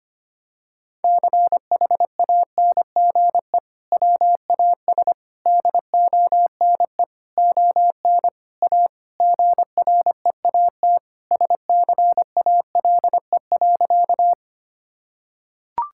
Morse code, heard as a keyed tone, about 25 wpm, 700 Hz. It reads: CHANGE WAS DONE ON A GREAT SCALE.
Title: Sentences from Top 1000 Words - Review Part 5 - 25wpm